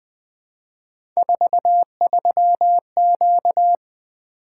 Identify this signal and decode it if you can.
Morse code, heard as a keyed tone, about 20 wpm, 700 Hz. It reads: 43Q
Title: Three Letter-Number Combinations - 20wpm